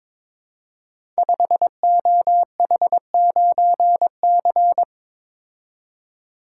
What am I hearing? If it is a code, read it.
Morse code, heard as a keyed tone, about 22 wpm, 700 Hz. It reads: 5OH9C
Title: Code Groups - 5 Letter:Numbers - 22wpm